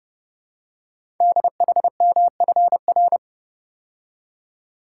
Morse code, heard as a keyed tone, about 30 wpm, 700 Hz. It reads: DHMFR
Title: Code Groups - 5 Letters - 30wpm